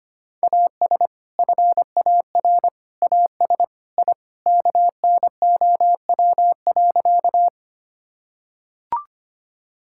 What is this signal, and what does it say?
Morse code, heard as a keyed tone, about 25 wpm, 700 Hz. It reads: AS FAR AS I KNOW.
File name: practice / 25wpm / Sentences with 3-Word Phrases - 25wpm